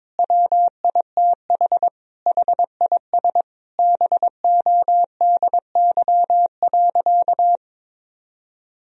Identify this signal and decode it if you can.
Morse code, heard as a keyed tone, about 22 wpm, 700 Hz. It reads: WITH HIS BODY.
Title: Sentences from Top 1000 Words - Review Part 2 - 22wpm